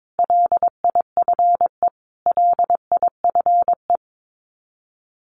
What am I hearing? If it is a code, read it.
Morse code, heard as a keyed tone, about 22 wpm, 700 Hz. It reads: LIFE LIFE
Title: Top 500 Words Repeated 22wpm